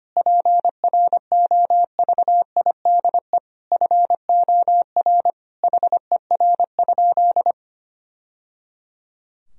Morse code, heard as a keyed tone, about 25 wpm, 700 Hz. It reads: PROVIDE FOR HER?
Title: Sentences from Top 1000 Words - Review Part 3 - 25wpm